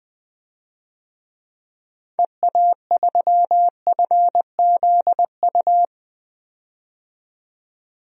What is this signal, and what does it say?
Morse code, heard as a keyed tone, about 21 wpm, 700 Hz. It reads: EA3FZU
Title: DXCC-100 Call Signs - 6 Characters - 20wpm